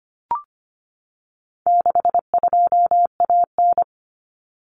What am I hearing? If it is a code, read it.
Morse code, heard as a keyed tone, about 25 wpm, 700 Hz. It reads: 62AN
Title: Common Cut Numbers - 4-Digit 25wpm